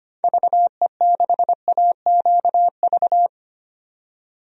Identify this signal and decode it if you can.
Morse code, heard as a keyed tone, about 25 wpm, 700 Hz. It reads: VE6AQV
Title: DXCC-100 Call Signs - 6 Characters - 25wpm